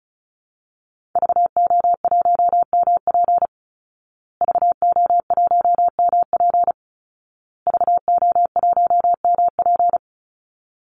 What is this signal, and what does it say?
Morse code, heard as a keyed tone, about 34 wpm, 700 Hz. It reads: VO1MP VO1MP VO1MP
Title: Call Signs Repeated 2x - 35wpm